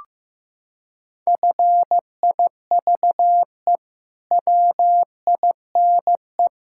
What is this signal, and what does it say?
Morse code, heard as a keyed tone, about 15 wpm, 700 Hz. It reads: FIVE WINE
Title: Pangram Challenge v2 - 15wpm